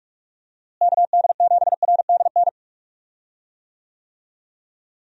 Morse code, heard as a keyed tone, about 46 wpm, 700 Hz. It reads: KD7RDN
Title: USA Call Signs - 6 Characters - Repeated 3x - 45wpm